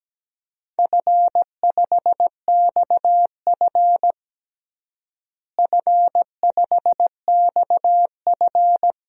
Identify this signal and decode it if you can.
Morse code, heard as a keyed tone, about 17 wpm, 700 Hz. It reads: F5XF F5XF
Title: DXCC-100 Call Signs - 4 Characters - Repeated 3x - 17wpm